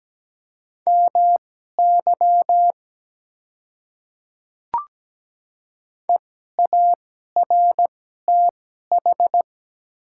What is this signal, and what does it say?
Morse code, heard as a keyed tone, about 17 wpm, spaced as for 12 wpm, 700 Hz. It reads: MY EARTH
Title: 2x Character Spacing - Top 200 Words - 17wpm